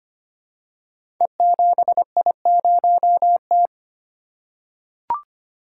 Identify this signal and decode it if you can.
Morse code, heard as a keyed tone, about 25 wpm, 700 Hz. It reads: E7I0T
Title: Code Groups - 5 Letter:Numbers - 25wpm